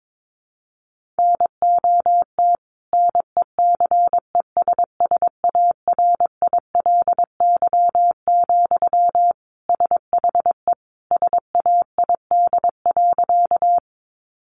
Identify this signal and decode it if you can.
Morse code, heard as a keyed tone, about 22 wpm, 700 Hz. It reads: NOT NECESSARILY, SHE SAID.
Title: Sentences from Top 1400 Words 22wpm